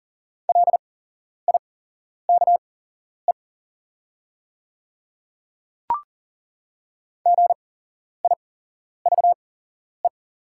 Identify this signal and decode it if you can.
Morse code, heard as a keyed tone, about 40 wpm, spaced as for 11 wpm, 690 Hz. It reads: LIKE GIVE
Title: 8x Character Spacing - Top 500 Words - 40wpm